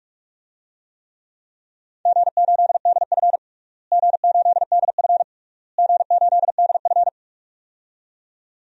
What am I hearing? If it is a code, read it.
Morse code, heard as a keyed tone, about 46 wpm, 700 Hz. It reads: G8DF G8DF G8DF
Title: DXCC-100 Call Signs - 4 Characters - Repeated 3x - 45wpm